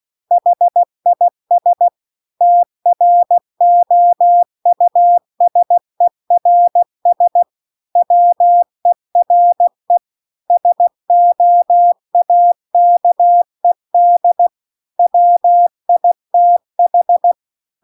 Morse code, heard as a keyed tone, about 16 wpm, 700 Hz. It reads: HIS TROUSERS WERE SOAKED WITH